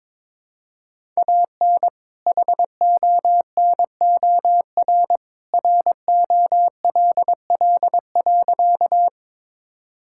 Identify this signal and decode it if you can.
Morse code, heard as a keyed tone, about 21 wpm, 700 Hz. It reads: AN HONOR ROLL.
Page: morse-code-ninja-practice-22wpm